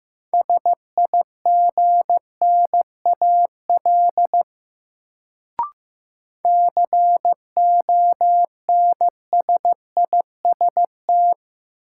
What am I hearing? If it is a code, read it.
Morse code, heard as a keyed tone, about 15 wpm, 700 Hz. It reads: SIGNAL CONSIST
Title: Sentences from Top 1400 Words - New Word Drill Alt - 15wpm